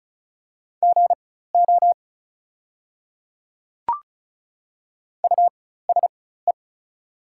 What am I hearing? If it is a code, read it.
Morse code, heard as a keyed tone, about 36 wpm, spaced as for 16 wpm, 700 Hz. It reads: GO USE
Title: 4x Character Spacing - Top 100 Words - 35wpm